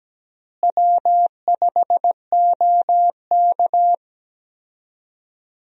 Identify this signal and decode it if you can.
Morse code, heard as a keyed tone, about 17 wpm, 700 Hz. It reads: W5OK
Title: USA Call Signs - 4 Characters - Repeated 2x - 17wpm